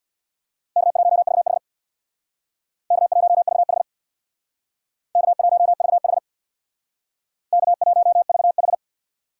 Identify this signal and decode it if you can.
Morse code, heard as a keyed tone, about 63 wpm, 700 Hz. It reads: K1VH K1VH K1VH K1VH